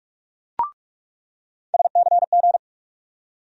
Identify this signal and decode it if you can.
Morse code, heard as a keyed tone, about 47 wpm, 700 Hz. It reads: SCG
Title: USA Call Sign Postfixes - 3 Letters 45wpm